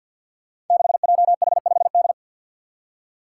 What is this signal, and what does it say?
Morse code, heard as a keyed tone, about 52 wpm, 700 Hz. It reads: 6JH5D
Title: Five Challenging Letter-Number Combinations 50wpm